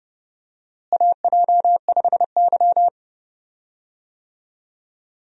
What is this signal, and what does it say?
Morse code, heard as a keyed tone, about 30 wpm, 700 Hz. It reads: AJ5Y
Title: USA Call Signs - 4 Characters - Repeated 3x - 30wpm